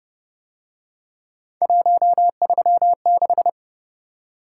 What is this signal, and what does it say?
Morse code, heard as a keyed tone, about 30 wpm, 700 Hz. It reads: 136